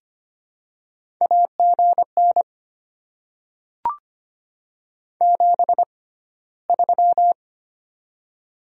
Morse code, heard as a keyed tone, about 25 wpm, spaced as for 15 wpm, 700 Hz. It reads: AGN 73
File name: Top 20 QSO Elements - 6x Character Spacing - 25wpm